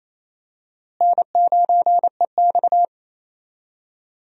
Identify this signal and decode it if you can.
Morse code, heard as a keyed tone, about 28 wpm, 700 Hz. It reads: N9EX
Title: USA Call Signs - 4 Characters - Repeated 3x - 28wpm